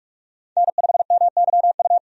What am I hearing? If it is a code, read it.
Morse code, heard as a keyed tone, about 44 wpm, 700 Hz. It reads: N5MYU
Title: USA Call Signs - 5 Characters - 45wpm